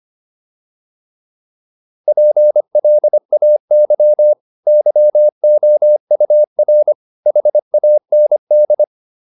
Morse code, heard as a keyed tone, about 25 wpm, 600 Hz. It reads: PLAY YOUR HAND
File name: PR302 25